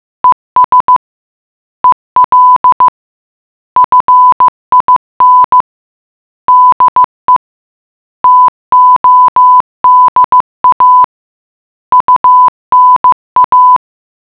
Morse code, heard as a keyed tone, about 15 wpm, 1000 Hz. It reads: ES EL FIN DE TODA UNA